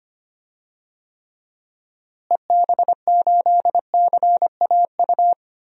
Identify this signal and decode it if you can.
Morse code, heard as a keyed tone, about 25 wpm, 700 Hz. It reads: EB8CAU
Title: DXCC-100 Call Signs - 6 Characters - Repeated 3x - 25wpm